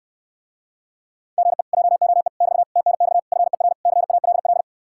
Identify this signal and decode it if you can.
Morse code, heard as a keyed tone, about 69 wpm, 700 Hz. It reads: GE JOE <BT> TNX FER CALL